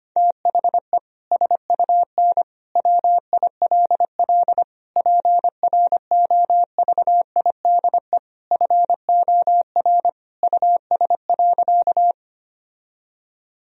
Morse code, heard as a keyed tone, about 25 wpm, 700 Hz. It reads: THE SUN WILL PROVIDE FOR US.
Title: Sentences from Top 1000 Words - Review Part 3 - 25wpm